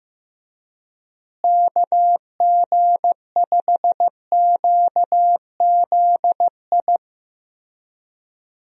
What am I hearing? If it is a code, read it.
Morse code, heard as a keyed tone, about 15 wpm, 700 Hz. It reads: KG5QZI
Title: USA Call Signs - 6 Characters - Repeated 3x - 15wpm